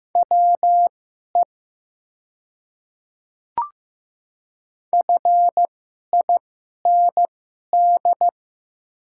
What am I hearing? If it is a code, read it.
Morse code, heard as a keyed tone, about 15 wpm, spaced as for 11 wpm, 700 Hz. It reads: WE FIND
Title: 2x Character Spacing - Top 200 Words - 15wpm